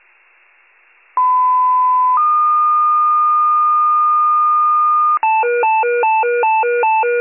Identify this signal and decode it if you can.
Two-tone page, heard as a one-second tone, then a three-second tone, about 1010 then 1220 Hz. It is heard from 1.2 to 5.2 s.